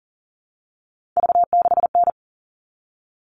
Morse code, heard as a keyed tone, about 40 wpm, 700 Hz. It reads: V6N
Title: All Cut Numbers - 3-Digit 40wpm